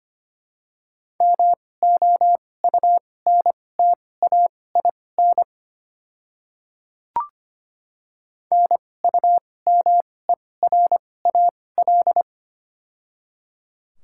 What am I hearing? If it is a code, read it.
Morse code, heard as a keyed tone, about 25 wpm, spaced as for 18 wpm, 700 Hz. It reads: MOUNTAIN NUMERAL
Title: 2x Character Spacing - Top 500 Words - 25wpm